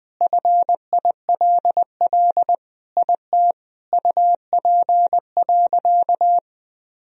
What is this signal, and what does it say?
Morse code, heard as a keyed tone, about 21 wpm, 700 Hz. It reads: FILL IT UP.
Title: Sentences from Top 1000 Words - Review Part 3 - 20wpm